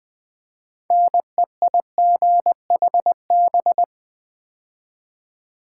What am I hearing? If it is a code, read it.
Morse code, heard as a keyed tone, about 20 wpm, 700 Hz. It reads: NEIGHB
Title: Build A Word - 3 Letters At A Time - 9-Letter Words - 20wpm